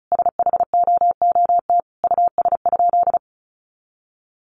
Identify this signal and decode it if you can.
Morse code, heard as a keyed tone, about 34 wpm, 700 Hz. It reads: SHOOT US?
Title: Sentences from Top 1200 Words 35wpm